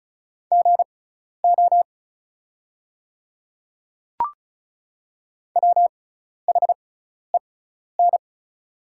Morse code, heard as a keyed tone, about 34 wpm, spaced as for 12 wpm, 700 Hz. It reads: GO WHEN
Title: 6x Character Spacing - Top 100 Words - 35wpm